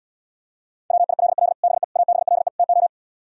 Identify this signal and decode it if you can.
Morse code, heard as a keyed tone, about 77 wpm, 700 Hz. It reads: WILL BE ABLE TO